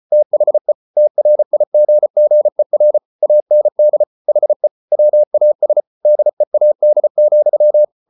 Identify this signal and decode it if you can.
Morse code, heard as a keyed tone, about 34 wpm, 600 Hz. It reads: THE TRIGGER AND HE WAS DEAD,